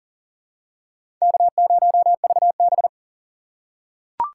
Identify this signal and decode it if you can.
Morse code, heard as a keyed tone, about 40 wpm, 700 Hz. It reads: K0VB
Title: USA Call Signs - 4 Characters - 3x Character Spacing - 40wpm